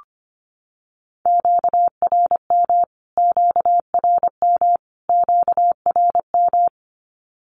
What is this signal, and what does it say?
Morse code, heard as a keyed tone, about 25 wpm, 700 Hz. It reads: QRM QRM QRM
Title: Q-Codes Repeated 3x 25wpm